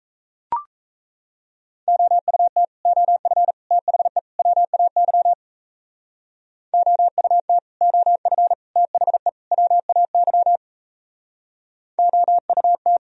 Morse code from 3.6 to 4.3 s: THE

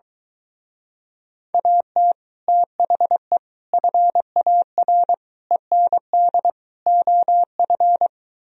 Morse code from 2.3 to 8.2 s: THE FAR END OF

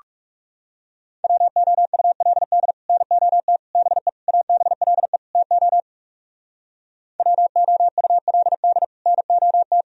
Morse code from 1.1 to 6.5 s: WOULD NOT BE ABLE TO

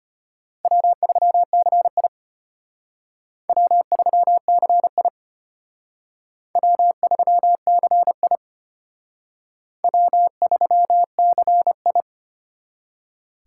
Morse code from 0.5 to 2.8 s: W3CI

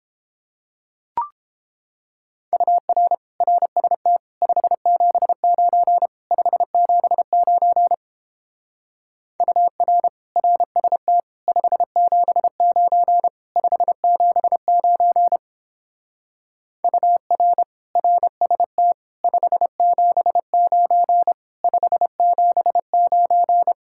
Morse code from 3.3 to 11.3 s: RST 579 579 UR RST

